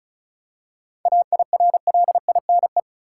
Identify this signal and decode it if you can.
Morse code, heard as a keyed tone, about 34 wpm, 700 Hz. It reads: AIRLINE
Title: Seven Letter Words - 35wpm